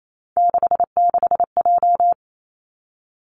Morse code, heard as a keyed tone, about 28 wpm, 700 Hz. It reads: 66J
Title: Three Challenging Letter-Number Combinations 28wpm